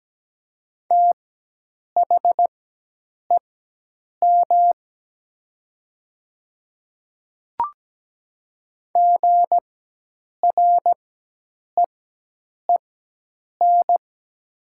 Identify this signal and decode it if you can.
Morse code, heard as a keyed tone, about 17 wpm, spaced as for 8 wpm, 700 Hz. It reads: THEM GREEN
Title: 4x Character Spacing - Top 500 Words - 17wpm